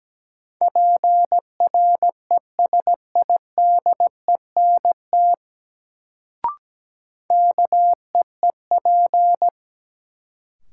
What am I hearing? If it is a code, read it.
Morse code, heard as a keyed tone, about 17 wpm, 700 Hz. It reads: PRESIDENT KEEP
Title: Sets of 2 Words - New Word Drill - 17wpm